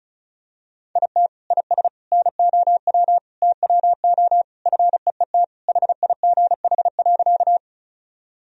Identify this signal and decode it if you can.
Morse code, heard as a keyed tone, about 36 wpm, 700 Hz. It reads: IT IS NOW TWO FEET HIGH.
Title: Sentences from Top 1000 Words - Review Part 1 - 35wpm